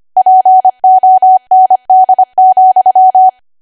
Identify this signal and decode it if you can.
Morse code, heard as a keyed tone, about 25 wpm, 750 Hz. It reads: POND,
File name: Archive / 25 WPM / 25WPM